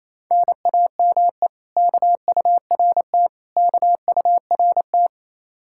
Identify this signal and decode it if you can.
Morse code, heard as a keyed tone, about 33 wpm, 700 Hz. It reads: NAME KURT KURT
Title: Speed-Racing Common QSO Phrases - 28wpm